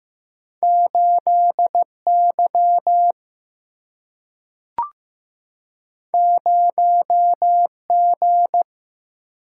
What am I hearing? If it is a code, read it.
Morse code, heard as a keyed tone, about 15 wpm, 700 Hz. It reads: 8 Y 0 G
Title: Single-Letter Single-Number 15wpm